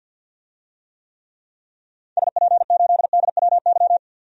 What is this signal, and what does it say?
Morse code, heard as a keyed tone, about 52 wpm, 700 Hz. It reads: SP8DWY
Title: DXCC-100 Call Signs - 6 Characters - 50wpm